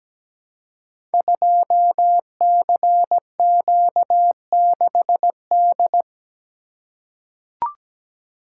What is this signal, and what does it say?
Morse code, heard as a keyed tone, about 17 wpm, 700 Hz. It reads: 2CQ6D
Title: Code Groups - 5 Letter:Numbers - 17wpm